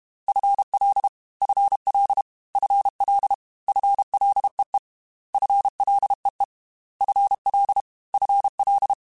Morse code, heard as a keyed tone, about 32 wpm, 800 Hz. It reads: FL FL FL FLEE FLEE FL FL
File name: FL / FL mp3s / FL Prefixes 3x2 @30